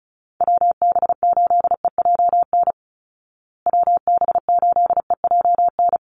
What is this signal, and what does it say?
Morse code, heard as a keyed tone, about 34 wpm, 700 Hz. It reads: WB8EJN WB8EJN
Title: Call Signs Repeated 3x - 35wpm